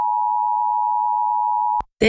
Emergency Alert System (SAME) tones (the 853 and 960 Hz attention tone). It is heard from 0.0 to 1.9 s.